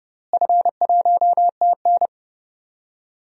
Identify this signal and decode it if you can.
Morse code, heard as a keyed tone, about 30 wpm, 700 Hz. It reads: F1TN
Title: DXCC-100 Call Signs - 4 Characters - Repeated 2x - 30wpm